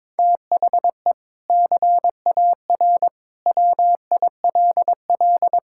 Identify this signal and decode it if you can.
Morse code, heard as a keyed tone, about 21 wpm, 700 Hz. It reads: THE CAR WILL
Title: Sentences from Top 1000 Words - Review Part 6 - 22wpm